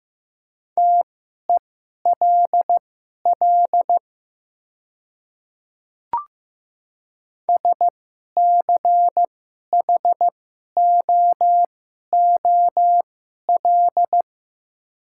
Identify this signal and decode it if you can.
Morse code, heard as a keyed tone, about 15 wpm, spaced as for 11 wpm, 700 Hz. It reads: TELL SCHOOL
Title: 2x Character Spacing - Top 300 Words - 15wpm